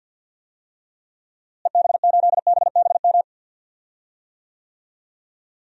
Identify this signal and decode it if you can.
Morse code, heard as a keyed tone, about 48 wpm, 700 Hz. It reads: EB8BBM
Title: DXCC-100 Call Signs - 6 Characters - Repeated 2x - 50wpm